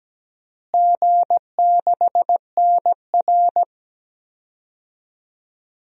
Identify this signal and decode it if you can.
Morse code, heard as a keyed tone, about 17 wpm, 700 Hz. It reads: G6NR
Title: DXCC-100 Call Signs - 4 Characters - Repeated 3x - 17wpm